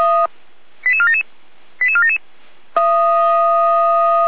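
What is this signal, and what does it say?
Five-tone selective call (ZVEI), heard as quick runs of five tones, tones of about 70 ms. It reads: 88477 88477
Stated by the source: ZVEI five-tone selective call 88475 received by SDR.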